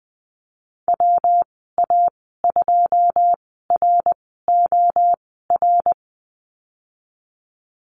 Morse code, heard as a keyed tone, about 20 wpm, spaced as for 15 wpm, 700 Hz. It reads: WA2ROR
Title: 167 Lesson 27g - Number 7 - Call Signs 20wpm